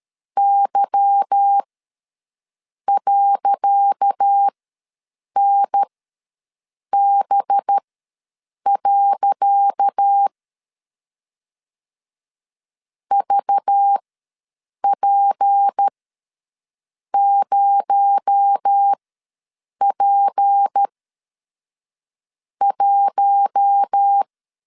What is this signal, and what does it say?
Morse code, heard as a keyed tone, about 13 wpm, spaced as for 6 wpm, 800 Hz. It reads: Y.NB. VP0P1
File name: LEZIONI / 10 SIGNS / 10C DOT ws 12wpm 3548